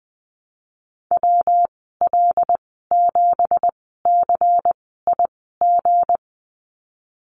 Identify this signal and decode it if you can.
Morse code, heard as a keyed tone, about 20 wpm, spaced as for 15 wpm, 700 Hz. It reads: WL7CIG
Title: 173 Lesson 28g - Number 9 - Call Signs 20wpm